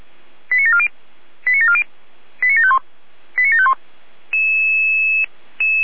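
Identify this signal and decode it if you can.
Five-tone selective call (ZVEI), heard as quick runs of five tones, tones of about 70 ms. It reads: 89740 89740 89751 89751
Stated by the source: ZVEI five-tone call received by SDR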